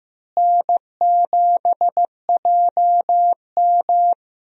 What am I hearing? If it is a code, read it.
Morse code, heard as a keyed tone, about 15 wpm, 700 Hz. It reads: N7JM
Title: USA Call Signs - 4 Characters - Repeated 3x - 15wpm